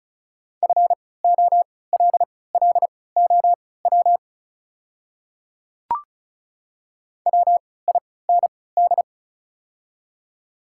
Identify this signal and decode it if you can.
Morse code, heard as a keyed tone, about 36 wpm, spaced as for 20 wpm, 690 Hz. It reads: FOLLOW WIND
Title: 3x Character Spacing - Top 500 Words - 35wpm